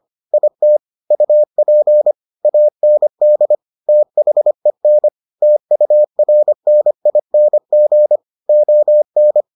Morse code, heard as a keyed tone, about 25 wpm, 600 Hz. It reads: IT UP AND THEN TURNING ON